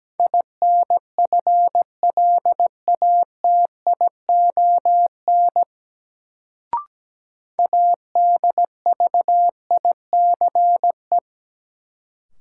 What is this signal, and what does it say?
Morse code, heard as a keyed tone, about 17 wpm, 700 Hz. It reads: INFLATION ADVICE
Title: Sentences from Top 1100 Words - New Word Drill - 17wpm